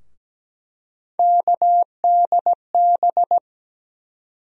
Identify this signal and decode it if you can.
Morse code, heard as a keyed tone, about 17 wpm, 700 Hz. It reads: KDB
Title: USA Call Sign Postfixes - 3 Letters 17wpm